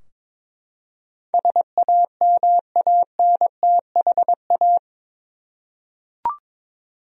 Morse code, heard as a keyed tone, about 22 wpm, 700 Hz. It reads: SAMANTHA
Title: USA Common Names - 22wpm